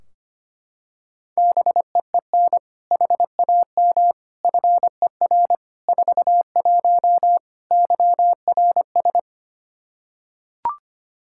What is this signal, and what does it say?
Morse code, heard as a keyed tone, about 25 wpm, 700 Hz. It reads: BEEN HAM FER 41 YRS
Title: K1USN SST QSO Practice 25wpm